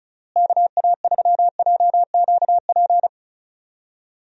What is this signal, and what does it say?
Morse code, heard as a keyed tone, about 34 wpm, 700 Hz. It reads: KA3JQP
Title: USA Call Signs - 6 Characters - Repeated 3x - 35wpm